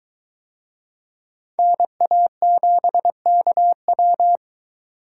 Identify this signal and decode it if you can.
Morse code, heard as a keyed tone, about 22 wpm, 700 Hz. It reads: NA7KW